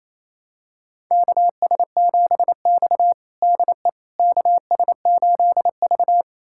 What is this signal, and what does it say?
Morse code, heard as a keyed tone, about 28 wpm, 700 Hz. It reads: KS7X DE KS8V
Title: USA Callsign DE Callsign - 4 Characters - 28wpm